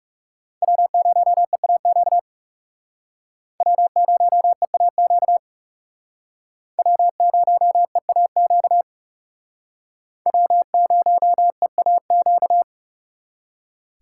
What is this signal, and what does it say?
Morse code, heard as a keyed tone, about 36 wpm, 700 Hz. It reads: W0EAQ W0EAQ W0EAQ W0EAQ
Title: Speed Racing - USA 1x3 Call Signs 30wpm